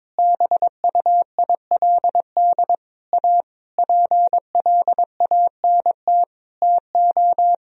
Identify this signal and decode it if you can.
Morse code, heard as a keyed tone, about 22 wpm, 700 Hz. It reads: BUILD A PLANT TO
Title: Sentences from Top 1000 Words - Review Part 1 - 22wpm